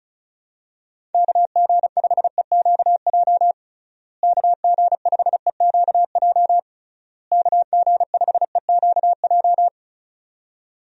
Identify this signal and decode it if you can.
Morse code, heard as a keyed tone, about 36 wpm, 700 Hz. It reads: KG5EQJ KG5EQJ KG5EQJ
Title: USA Call Signs - 6 Characters - Repeated 3x - 35wpm